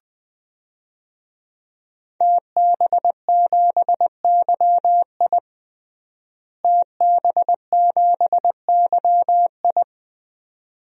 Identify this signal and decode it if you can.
Morse code, heard as a keyed tone, about 20 wpm, 700 Hz. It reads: TB7YI TB7YI
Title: DXCC-100 Call Signs - 5 Characters - Repeated 2x - 20wpm